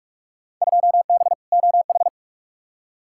Morse code, heard as a keyed tone, about 41 wpm, 700 Hz. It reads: 2B OH
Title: Field Day Exchanges - 45wpm